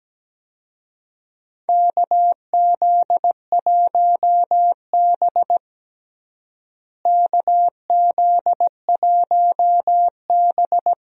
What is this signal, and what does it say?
Morse code, heard as a keyed tone, about 17 wpm, 700 Hz. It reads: KZ1B KZ1B
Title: USA Call Signs - 4 Characters - Repeated 3x - 17wpm